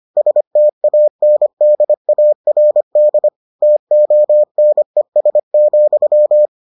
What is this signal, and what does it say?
Morse code, heard as a keyed tone, about 25 wpm, 600 Hz. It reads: STANDARD TONES,